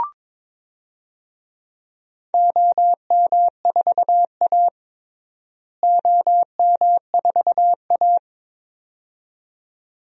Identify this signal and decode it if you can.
Morse code, heard as a keyed tone, about 22 wpm, 700 Hz. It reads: OM4A OM4A
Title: DXCC-100 Call Signs - 4 Characters - Repeated 2x - 22wpm